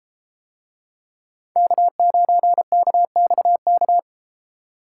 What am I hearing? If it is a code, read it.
Morse code, heard as a keyed tone, about 32 wpm, 700 Hz. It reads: K9KXK